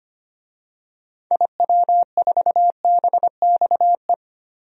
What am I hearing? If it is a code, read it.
Morse code, heard as a keyed tone, about 25 wpm, 700 Hz. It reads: IW4BXE
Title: DXCC-100 Call Signs - 6 Characters - 25wpm